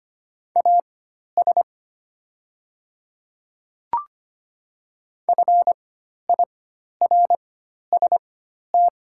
Morse code, heard as a keyed tone, about 25 wpm, spaced as for 12 wpm, 700 Hz. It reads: AS FIRST
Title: 4x Character Spacing - Top 100 Words - 25wpm